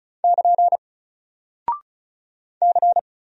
Morse code, heard as a keyed tone, about 35 wpm, 700 Hz. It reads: <KN> C
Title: Single Character (Letter:Number:Punctuation:Prosign) 35wpm